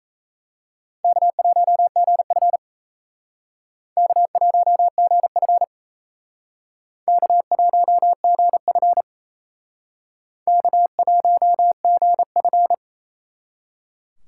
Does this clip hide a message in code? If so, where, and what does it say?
Yes: Morse code from 1.0 to 12.7 s, heard as a keyed tone, about 37 wpm, 700 Hz. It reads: K1GF K1GF K1GF K1GF